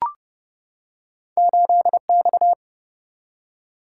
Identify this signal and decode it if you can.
Morse code, heard as a keyed tone, about 31 wpm, 700 Hz. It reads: 8X
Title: Single-Letter Single-Number 30wpm